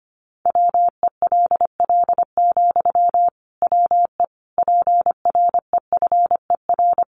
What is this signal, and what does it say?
Morse code, heard as a keyed tone, about 25 wpm, 700 Hz. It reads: WELL, WE PREFER
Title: Sentences from Top 1100 Words 25wpm